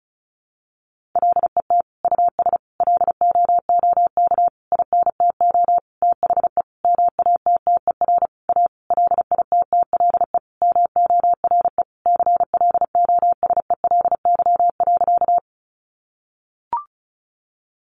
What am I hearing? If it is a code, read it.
Morse code, heard as a keyed tone, about 34 wpm, 700 Hz. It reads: LET US LOOK INTO THE MATTER A LITTLE MORE CLOSELY.
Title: Sentences from Top 1400 Words 35wpm